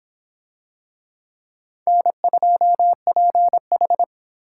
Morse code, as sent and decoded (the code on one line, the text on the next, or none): -. ..--- .--. ....
N2PH